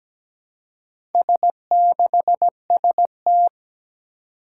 Morse code, heard as a keyed tone, about 17 wpm, 700 Hz. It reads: S6ST
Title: Four Challenging Letter-Number Combinations 17wpm